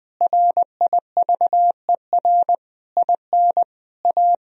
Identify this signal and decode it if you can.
Morse code, heard as a keyed tone, about 20 wpm, 700 Hz. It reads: RIVER IN A
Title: Sentences from Top 1000 Words - Review Part 3 - 20wpm